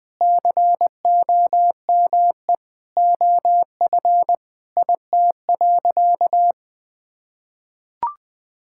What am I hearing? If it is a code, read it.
Morse code, heard as a keyed tone, about 20 wpm, 700 Hz. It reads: COME OF IT.
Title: Sentences with 2-Word Phrases - Spoken Beginning - 20wpm